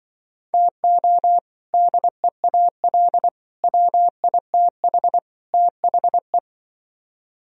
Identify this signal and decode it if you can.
Morse code, heard as a keyed tone, about 24 wpm, 700 Hz. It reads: TO DEAL WITH THE
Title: Speed-Racing - Sets of 4 Words 20wpm